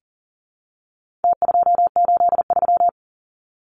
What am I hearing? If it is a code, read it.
Morse code, heard as a keyed tone, about 40 wpm, 700 Hz. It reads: T283
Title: Numbers - Contest Style - 4-Digit 40wpm